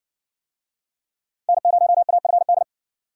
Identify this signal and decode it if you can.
Morse code, heard as a keyed tone, about 54 wpm, 700 Hz. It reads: N0AFD